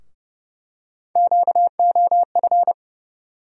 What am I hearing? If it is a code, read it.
Morse code, heard as a keyed tone, about 30 wpm, 700 Hz. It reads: QOF
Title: USA Call Sign Postfixes - 3 Letters 30wpm